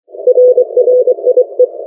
Morse code, heard as a keyed tone, about 24 wpm, 500 Hz. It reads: RRIE